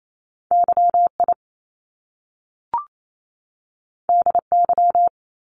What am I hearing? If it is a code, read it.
Morse code, heard as a keyed tone, about 28 wpm, 700 Hz. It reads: YI DY
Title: Top Two Letter Combinations in Words 28wpm